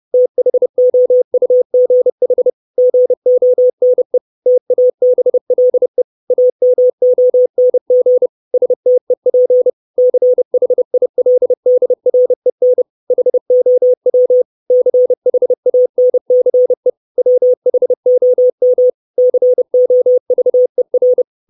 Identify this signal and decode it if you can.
Morse code, heard as a keyed tone, about 30 wpm, 500 Hz. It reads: THOUGH GONE TABLE AMONG STEP CHILDREN HOW CHANCE WHOM COVER
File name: 632 Words 30WPM